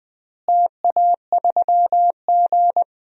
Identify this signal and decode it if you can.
Morse code, heard as a keyed tone, about 21 wpm, 700 Hz. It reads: TA3G
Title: DXCC-100 Call Signs - 4 Characters - Repeated 3x - 20wpm